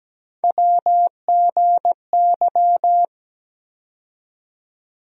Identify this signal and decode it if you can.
Morse code, heard as a keyed tone, about 17 wpm, 700 Hz. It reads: WGY
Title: USA Call Sign Postfixes - 3 Letters 17wpm